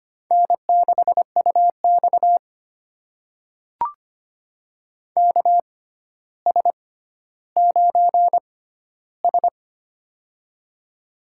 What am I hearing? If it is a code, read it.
Morse code, heard as a keyed tone, about 25 wpm, spaced as for 13 wpm, 700 Hz. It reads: N6UX KS9S
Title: USA Call Signs - 4 Characters - 6x Character Spacing - 25wpm